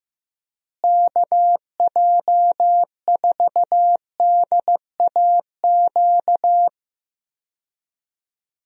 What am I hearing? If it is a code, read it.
Morse code, heard as a keyed tone, about 15 wpm, 700 Hz. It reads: KJ4DAQ
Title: USA Call Signs - 6 Characters - Repeated 3x - 15wpm